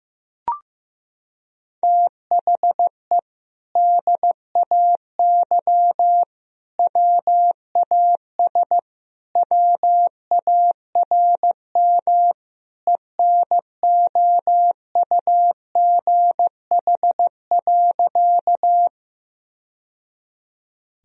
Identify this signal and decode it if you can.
Morse code, heard as a keyed tone, about 16 wpm, 700 Hz. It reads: THE DAY WAS WARM ENOUGH.
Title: Sentences from Top 1000 Words - Review Part 2 - 15wpm